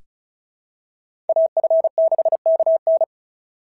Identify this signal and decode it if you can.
Morse code, heard as a keyed tone, about 36 wpm, 690 Hz. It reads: AF6KN
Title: SOTA QSO Practice 35wpm